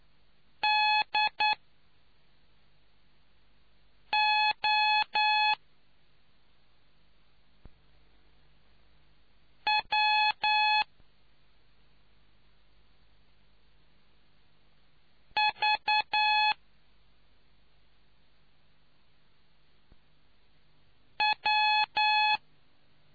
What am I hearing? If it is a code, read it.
Morse code, heard as a keyed tone, about 9 wpm, spaced as for 2 wpm, 850 Hz. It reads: DOWVW